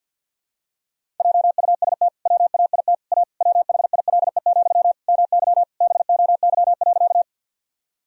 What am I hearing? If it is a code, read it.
Morse code, heard as a keyed tone, about 49 wpm, 700 Hz. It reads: JUST WAIT A WHILE, MY BOY.
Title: Sentences from Top 1000 Words - Review Part 1 - 50wpm